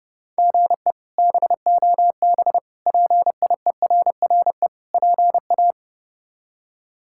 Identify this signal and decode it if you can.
Morse code, heard as a keyed tone, about 30 wpm, 700 Hz. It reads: GE BOB PIERRE PA
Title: K1USN SST QSO Practice - 30wpm